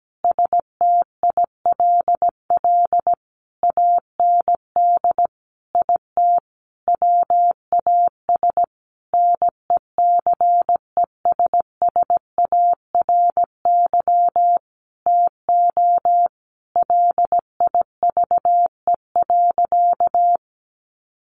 Morse code, heard as a keyed tone, about 17 wpm, 700 Hz. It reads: STILL AND IT WAS NECESSARY TO LIVE.
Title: Sentences from Top 800 Words 17wpm